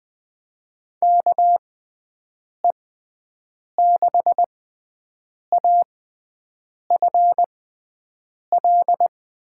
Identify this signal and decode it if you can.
Morse code, heard as a keyed tone, about 21 wpm, spaced as for 7 wpm, 700 Hz. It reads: KE6AFL
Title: USA Call Signs - 6x Character Spacing - 20wpm